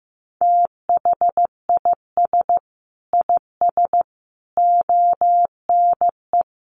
Morse code, heard as a keyed tone, about 15 wpm, 700 Hz. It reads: THIS IS ONE
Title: Sets of 5 Words Encore2 15wpm